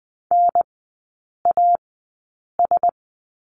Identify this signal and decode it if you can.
Morse code, heard as a keyed tone, about 20 wpm, spaced as for 8 wpm, 700 Hz. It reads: NAS
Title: 035 Lesson 7e - Letter s - Three Characters 20wpm